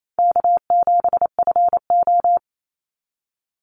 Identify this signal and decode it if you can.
Morse code, heard as a keyed tone, about 28 wpm, 700 Hz. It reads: K7FO
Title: NA SOTA Chaser Callsigns 28wpm